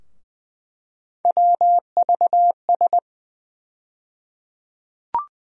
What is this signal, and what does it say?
Morse code, heard as a keyed tone, about 20 wpm, 700 Hz. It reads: WVS
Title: USA Call Sign Postfixes - 3 Letters 20wpm